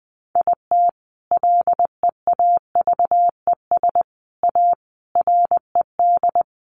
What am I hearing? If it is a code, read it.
Morse code, heard as a keyed tone, about 20 wpm, 700 Hz. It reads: IT LEAVES A RED
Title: A Sound Of Thunder 20wpm